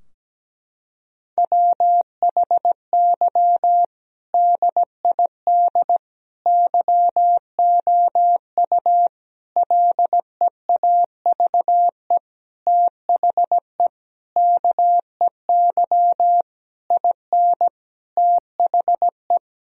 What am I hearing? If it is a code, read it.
Morse code, heard as a keyed tone, about 17 wpm, 700 Hz. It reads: WHY DID YOU LEAVE THE KEY IN THE
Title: Sentences from Top 1000 Words - Review Part 4 - 17wpm